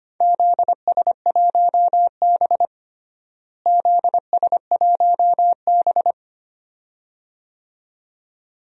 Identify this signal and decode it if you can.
Morse code, heard as a keyed tone, about 25 wpm, 700 Hz. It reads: ZS1B ZS1B
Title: DXCC-100 Call Signs - 4 Characters - Repeated 2x - 25wpm